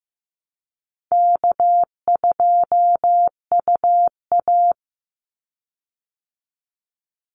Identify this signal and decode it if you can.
Morse code, heard as a keyed tone, about 15 wpm, 700 Hz. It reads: K2UA
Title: Easy Call Signs Repeated 2x - 15wpm